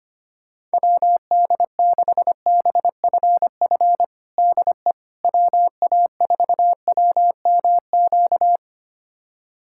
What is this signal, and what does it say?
Morse code, heard as a keyed tone, about 25 wpm, 700 Hz. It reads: WD6BFF DE WA4WMQ
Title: USA Callsign DE Callsign - 6 Characters - 25wpm